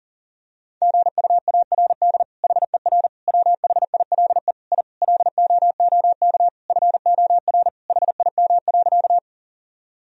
Morse code, heard as a keyed tone, about 37 wpm, 700 Hz. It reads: GUARD HER WHILE I LOOK FOR HIM.
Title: Sentences from Top 1000 Words - Review Part 4 - 40wpm